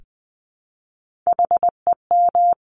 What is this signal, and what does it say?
Morse code, heard as a keyed tone, about 20 wpm, 700 Hz. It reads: HEM
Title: Top Three Letter Combinations in Words 20wpm